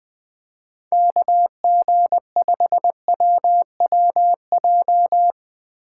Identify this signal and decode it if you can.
Morse code, heard as a keyed tone, about 22 wpm, 700 Hz. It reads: KG5WWJ
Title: Speed Racing - USA 2x3 Call Signs 20wpm